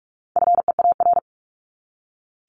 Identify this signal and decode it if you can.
Morse code, heard as a keyed tone, about 44 wpm, 700 Hz. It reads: FEAR
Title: Top 1751-2000 Words 45wpm